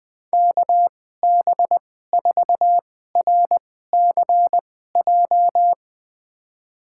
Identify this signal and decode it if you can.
Morse code, heard as a keyed tone, about 21 wpm, spaced as for 15 wpm, 700 Hz. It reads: KB4RCJ
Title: USA Call Signs - 6 Characters - 2x Character Spacing - 20wpm